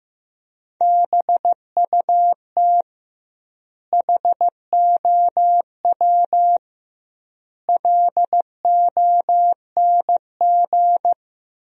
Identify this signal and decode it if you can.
Morse code, heard as a keyed tone, about 15 wpm, 700 Hz. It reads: BUT HOW LONG
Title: 2x Word-Spacing - Sentences from Top 500 Words - 15wpm